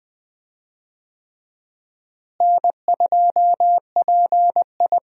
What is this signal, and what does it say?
Morse code, heard as a keyed tone, about 20 wpm, 700 Hz. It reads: N2PI
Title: USA Call Signs - 4 Characters - Repeated 3x - 20wpm